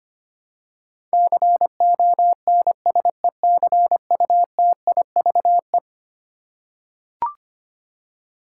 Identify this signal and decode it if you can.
Morse code, heard as a keyed tone, about 25 wpm, 700 Hz. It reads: CONSECUTIVE
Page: morse-code-ninja-practice-25wpm